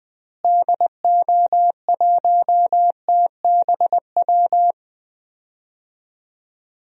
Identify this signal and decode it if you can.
Morse code, heard as a keyed tone, about 20 wpm, 700 Hz. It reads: DO1TBW
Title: DXCC-100 Call Signs - 6 Characters - Repeated 2x - 20wpm